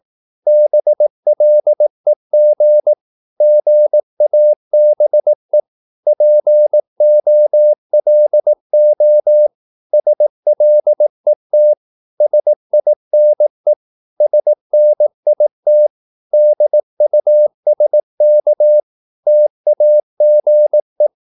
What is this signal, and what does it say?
Morse code, heard as a keyed tone, about 18 wpm, 600 Hz. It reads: BLEG GABE POLO SLET SINE SNIT DUSK TAGE
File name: Korte ord DK 18wpm